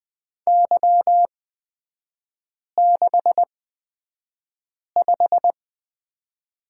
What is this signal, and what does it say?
Morse code, heard as a keyed tone, about 20 wpm, spaced as for 5 wpm, 700 Hz. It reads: Y65
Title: 224 Lesson 36e - BT - Three Characters 20wpm